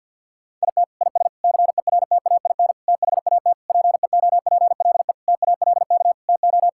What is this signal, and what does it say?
Morse code, heard as a keyed tone, about 50 wpm, 700 Hz. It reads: IT IS CERTAIN THAT PEOPLE TALK TO
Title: Sentences from Top 1000 Words - Review Part 5 - 50wpm